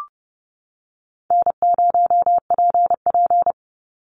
Morse code, heard as a keyed tone, about 30 wpm, 700 Hz. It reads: N0PP
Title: Call Signs - 30wpm